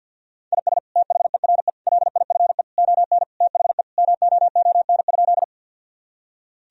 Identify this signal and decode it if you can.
Morse code, heard as a keyed tone, about 50 wpm, 700 Hz. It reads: IS THERE LIFE ON THE MOON?
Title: Sentences from Top 1000 Words - Review Part 3 - 50wpm